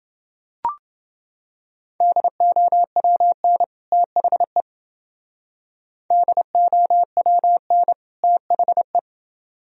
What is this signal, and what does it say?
Morse code, heard as a keyed tone, about 29 wpm, 700 Hz. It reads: DOWN THE DOWN THE